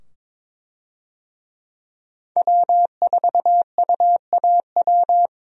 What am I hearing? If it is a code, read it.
Morse code, heard as a keyed tone, about 22 wpm, 700 Hz. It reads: W4UAW
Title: USA Call Signs - 5 Characters - 22wpm